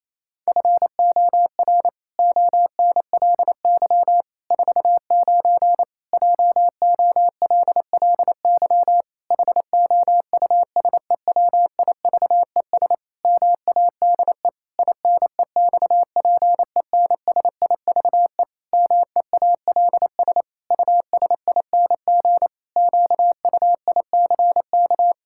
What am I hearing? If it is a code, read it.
Morse code, heard as a keyed tone, about 28 wpm, 700 Hz. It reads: FOR ONLY 49 JOLLY HOUSEWIVES MADE INEXPENSIVE MEALS USING QUICK
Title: Pangram Challenge v2 - 28wpm